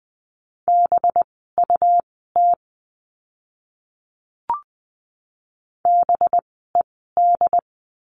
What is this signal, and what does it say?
Morse code, heard as a keyed tone, about 20 wpm, spaced as for 15 wpm, 700 Hz. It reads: BUT BED
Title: 185 Lesson 30f - Letter b - Words 20wpm